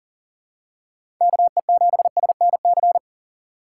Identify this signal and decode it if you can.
Morse code, heard as a keyed tone, about 40 wpm, 700 Hz. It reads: KE7SNC
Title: USA Call Signs - 3x Character Spacing - 40wpm